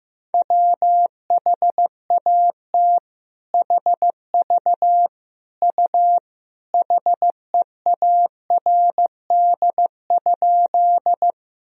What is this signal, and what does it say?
Morse code, heard as a keyed tone, about 15 wpm, 700 Hz. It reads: WHAT HV U HEARD?
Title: Sentences from Top 400 Words and Abbreviations 15wpm